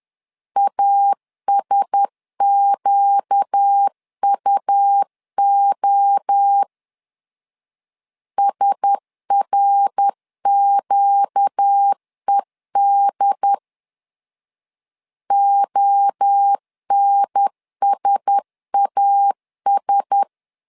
Morse code, heard as a keyed tone, about 10 wpm, 800 Hz. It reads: ASQUO SRQED ONSAS
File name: LEZIONI / 4 Q1S 10wpm 2133